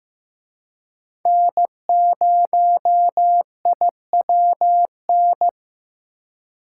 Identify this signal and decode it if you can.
Morse code, heard as a keyed tone, about 18 wpm, 700 Hz. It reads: N0IWN
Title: Speed Racing - USA Call Signs 15wpm